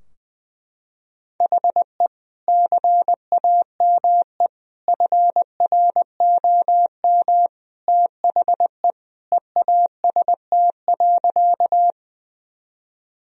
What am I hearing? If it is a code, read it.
Morse code, heard as a keyed tone, about 20 wpm, 700 Hz. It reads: HE CAME FROM THE EAST.
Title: Sentences from Top 1000 Words - Review Part 3 - 20wpm